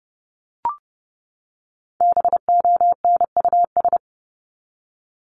Morse code, heard as a keyed tone, about 30 wpm, 700 Hz. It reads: BONUS
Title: Top 1751-2000 Words 30wpm